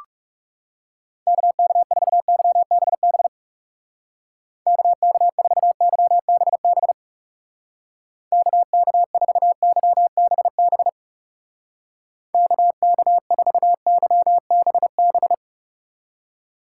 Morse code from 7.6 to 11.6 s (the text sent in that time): KK4YBB